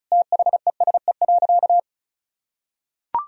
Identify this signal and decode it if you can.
Morse code, heard as a keyed tone, about 34 wpm, 700 Hz. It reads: THESE.
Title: Sentences from Top 1000 Words - Review Part 1 - 35wpm